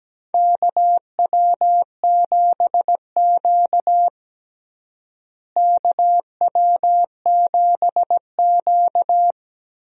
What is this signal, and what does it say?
Morse code, heard as a keyed tone, about 17 wpm, 700 Hz. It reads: KW7Q KW7Q
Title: USA Call Signs - 4 Characters - Repeated 2x - 17wpm